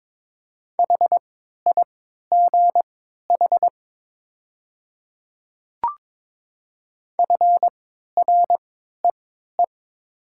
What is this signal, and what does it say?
Morse code, heard as a keyed tone, about 22 wpm, spaced as for 12 wpm, 700 Hz. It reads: HIGH FREE
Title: 3x Character Spacing - Top 500 Words - 22wpm